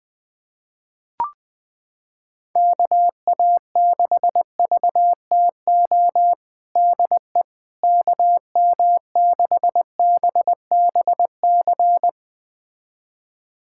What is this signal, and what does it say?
Morse code, heard as a keyed tone, about 20 wpm, 700 Hz. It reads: KA6VTO DE KM6BBC
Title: USA Callsign DE Callsign 20wpm